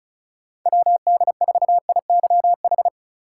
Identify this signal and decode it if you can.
Morse code, heard as a keyed tone, about 36 wpm, 700 Hz. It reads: WD4IYH
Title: USA Call Signs - 6 Characters - 3x Character Spacing - 35wpm